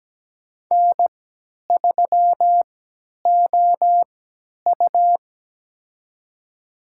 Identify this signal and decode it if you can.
Morse code, heard as a keyed tone, about 17 wpm, spaced as for 10 wpm, 700 Hz. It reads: N3OU
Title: USA Call Signs - 3x Character Spacing - 17wpm